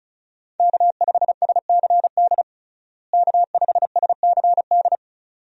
Morse code, heard as a keyed tone, about 36 wpm, 700 Hz. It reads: K5SCD K5SCD
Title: USA Call Signs - 5 Characters - Repeated 2x - 35wpm